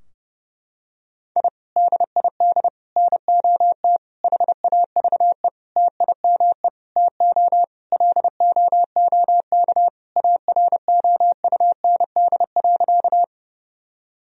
Morse code, heard as a keyed tone, about 32 wpm, 700 Hz. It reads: I DID NOT HAVE TIME TO LOOK AROUND.
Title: Sentences from Top 1000 Words - Review Part 3 - 30wpm